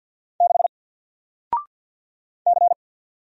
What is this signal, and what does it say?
Morse code, heard as a keyed tone, about 50 wpm, 700 Hz. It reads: 6 C
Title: Single Letter-Number 50wpm